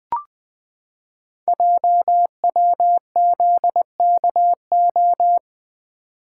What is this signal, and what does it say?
Morse code, heard as a keyed tone, about 20 wpm, 700 Hz. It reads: JWZKO
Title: Code Groups - 5 Letters - 20wpm